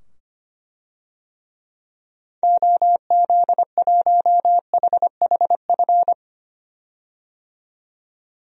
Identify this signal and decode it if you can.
Morse code, heard as a keyed tone, about 25 wpm, 700 Hz. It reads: OZ1HHF
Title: DXCC-100 Call Signs - 6 Characters - 25wpm